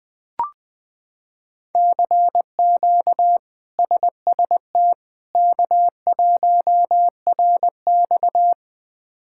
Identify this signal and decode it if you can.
Morse code, heard as a keyed tone, about 20 wpm, 700 Hz. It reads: CQ SST K1RX
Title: K1USN SST QSO Practice - 20wpm